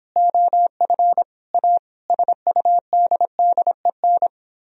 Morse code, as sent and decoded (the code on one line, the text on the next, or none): --- ..-. / .- / ... ..- -.. -.. -.
OF A SUDDEN